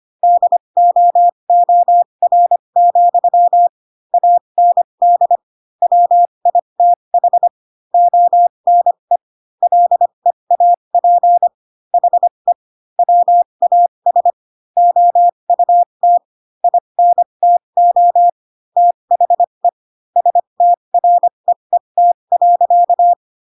Morse code, heard as a keyed tone, about 25 wpm, spaced as for 22 wpm, 700 Hz. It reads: DOOR, AND WITH ONE LEAP HE WAS OUT INTO THE STREET.
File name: Pinocchio / Pinocchio ch3 18 25 ik1hge